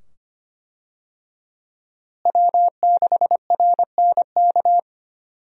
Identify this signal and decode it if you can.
Morse code, heard as a keyed tone, about 25 wpm, 700 Hz. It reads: W6RNK
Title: USA Call Signs - 5 Characters - Repeated 2x - 25wpm